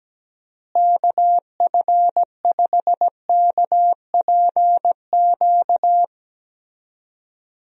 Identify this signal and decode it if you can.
Morse code, heard as a keyed tone, about 17 wpm, 700 Hz. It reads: KF5KPQ
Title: USA Call Signs - 6 Characters - Repeated 2x - 17wpm